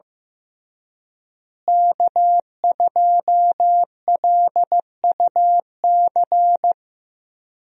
Morse code, heard as a keyed tone, about 15 wpm, 700 Hz. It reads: K2LUC
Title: USA Call Signs - 5 Characters - Repeated 3x - 15wpm